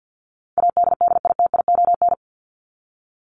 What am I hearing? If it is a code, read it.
Morse code, heard as a keyed tone, about 48 wpm, 700 Hz. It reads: ADDITION
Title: Top 751-1000 Words 50wpm